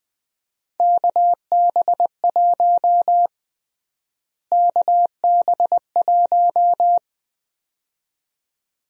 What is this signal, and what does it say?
Morse code, heard as a keyed tone, about 20 wpm, 700 Hz. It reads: KB1 KB1
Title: USA Call Sign Prefixes - Repeated 2x - 20wpm